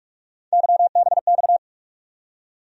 Morse code, heard as a keyed tone, about 46 wpm, 700 Hz. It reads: YBX
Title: USA Call Sign Postfixes - 3 Letters 45wpm